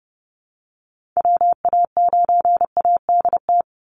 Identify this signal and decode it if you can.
Morse code, heard as a keyed tone, about 30 wpm, 700 Hz. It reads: WA9ADT
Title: Call Signs Repeated 3x - 30wpm